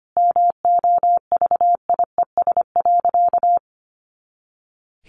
Morse code, heard as a keyed tone, about 25 wpm, 700 Hz. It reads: MOVIES.
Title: Additional Easy Phrases 25wpm